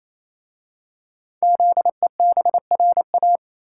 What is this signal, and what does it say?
Morse code, heard as a keyed tone, about 27 wpm, 700 Hz. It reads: ZEBRA
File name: Easy Words - Speed-Racing with Hints - 22wpm